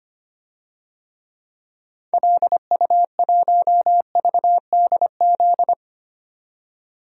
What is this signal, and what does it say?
Morse code, heard as a keyed tone, about 25 wpm, 690 Hz. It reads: LU1VDZ
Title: DXCC-100 Call Signs - 6 Characters - 25wpm